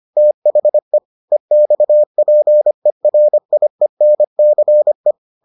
Morse code, heard as a keyed tone, about 25 wpm, 600 Hz. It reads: THE EXPERIENCE